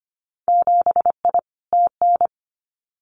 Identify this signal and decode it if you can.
Morse code, heard as a keyed tone, about 25 wpm, 700 Hz. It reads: 7I TN
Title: Winter Field Day Exchanges Encore 25wpm